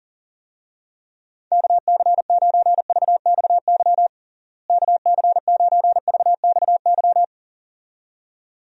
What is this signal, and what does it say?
Morse code, heard as a keyed tone, about 40 wpm, 700 Hz. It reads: KC9VXY KC9VXY
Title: USA Call Signs - 6 Characters - Repeated 2x - 40wpm